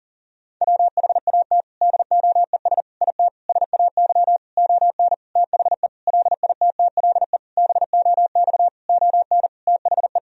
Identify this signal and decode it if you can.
Morse code, heard as a keyed tone, about 40 wpm, 700 Hz. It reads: WHAT DOES IT SAY ON THE LITTLE BOX ON THE
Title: Sentences from Top 1000 Words - Review Part 2 - 40wpm